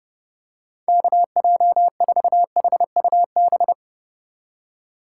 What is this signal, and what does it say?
Morse code, heard as a keyed tone, about 32 wpm, 700 Hz. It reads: KJ4HUB
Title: USA Call Signs - 6 Characters - 2x Character Spacing - 30wpm